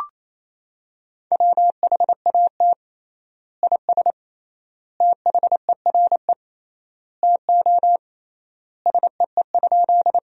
Morse code, heard as a keyed tone, about 28 wpm, 700 Hz. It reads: WHAT IS THERE TO SEE?
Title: 3x Word-Spacing - Sentences from Top 100 Words - 28wpm